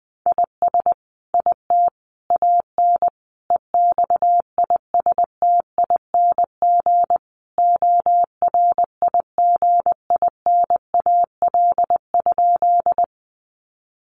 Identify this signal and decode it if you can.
Morse code, heard as a keyed tone, about 20 wpm, 700 Hz. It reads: IS IT AN EXISTING ORIGINAL?
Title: Sentences from Top 1500 Words 20wpm